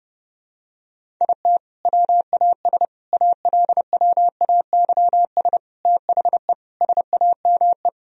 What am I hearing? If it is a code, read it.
Morse code, heard as a keyed tone, about 29 wpm, 700 Hz. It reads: IT WAS ALWAYS THE SAME
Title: Sentences from Top 1000 Words - Review Part 1 - 30wpm